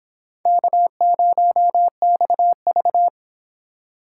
Morse code, heard as a keyed tone, about 26 wpm, 700 Hz. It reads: K0XV